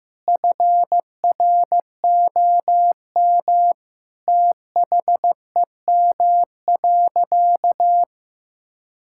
Morse code, heard as a keyed tone, about 15 wpm, 700 Hz. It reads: FROM THEM.
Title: Sentences from Top 1600 Words 15wpm